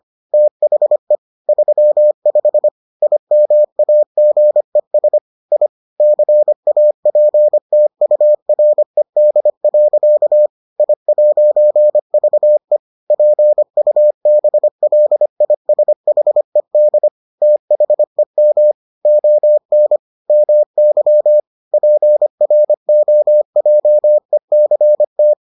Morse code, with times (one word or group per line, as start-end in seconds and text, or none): 0.3-1.1 THE
1.5-2.7 35
3.0-5.2 IMAGES
5.5-5.7 I
6.0-10.5 CAPTURED.
10.8-12.8 I'VE
13.1-17.1 PUBLISHED
17.4-18.7 THEM
19.1-20.0 ON
20.3-21.4 MY
21.7-25.3 PROJECT